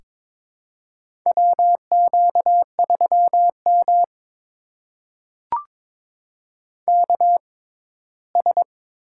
Morse code, heard as a keyed tone, about 22 wpm, spaced as for 15 wpm, 700 Hz. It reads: WQ3M KS
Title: USA Call Signs - 4 Characters - 6x Character Spacing - 22wpm